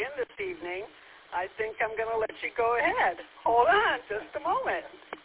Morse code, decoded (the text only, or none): none